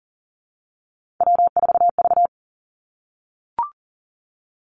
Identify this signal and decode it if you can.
Morse code, heard as a keyed tone, about 42 wpm, 700 Hz. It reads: W4V
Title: Three Challenging Letter-Number Combinations 40wpm